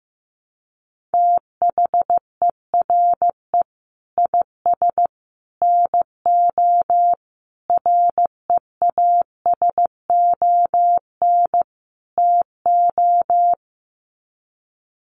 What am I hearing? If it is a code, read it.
Morse code, heard as a keyed tone, about 15 wpm, 700 Hz. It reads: THERE IS NO REASON TO
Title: Sets of 5 Words Encore 15wpm